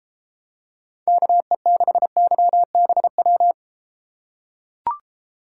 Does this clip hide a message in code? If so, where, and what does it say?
Yes: Morse code from 1.1 to 3.5 s, heard as a keyed tone, about 31 wpm, 700 Hz. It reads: KE6YBW